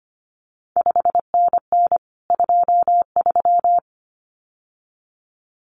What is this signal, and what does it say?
Morse code, heard as a keyed tone, about 25 wpm, 700 Hz. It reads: 5NN 23
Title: CQ WW DX Contest Exchange Encore - 25wpm